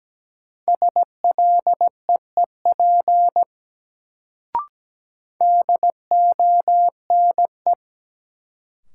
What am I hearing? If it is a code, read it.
Morse code, heard as a keyed tone, about 17 wpm, 700 Hz. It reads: SLEEP DONE
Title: Sentences from Top 500 Words - New Word Drill - 17wpm